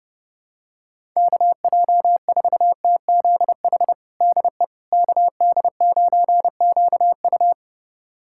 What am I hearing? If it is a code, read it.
Morse code, heard as a keyed tone, about 30 wpm, 700 Hz. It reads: KJ4TZH DE KD9QU
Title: USA Callsign DE Callsign 30wpm